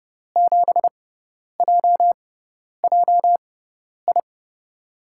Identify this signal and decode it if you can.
Morse code, heard as a keyed tone, about 30 wpm, spaced as for 10 wpm, 700 Hz. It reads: 7JJI
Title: USA Call Signs - 6 Characters - 6x Character Spacing - 30wpm